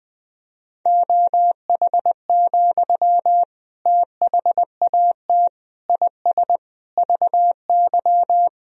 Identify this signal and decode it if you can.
Morse code, heard as a keyed tone, about 20 wpm, 700 Hz. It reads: OH, THAT IS VY
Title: Sentences from Top 500 Words and Abbreviations 20wpm